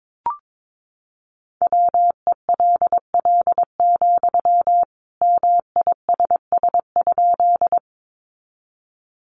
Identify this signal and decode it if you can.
Morse code, heard as a keyed tone, about 22 wpm, 700 Hz. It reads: WELL, MISS?
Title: Sentences from Top 500 Words 22wpm